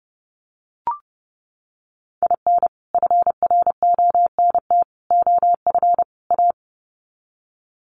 Morse code, heard as a keyed tone, about 30 wpm, 700 Hz. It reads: IN FRONT OF A
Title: Sets of 4 Words 30wpm